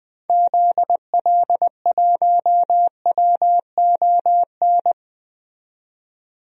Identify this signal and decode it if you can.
Morse code, heard as a keyed tone, about 21 wpm, 700 Hz. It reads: ZL1WON
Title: DXCC-100 Call Signs - 6 Characters - 20wpm